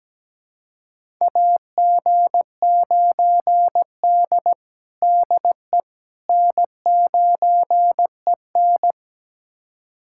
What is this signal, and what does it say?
Morse code, heard as a keyed tone, about 17 wpm, 700 Hz. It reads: AG9D DE N9EN
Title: USA Callsign DE Callsign - 4 Characters - 17wpm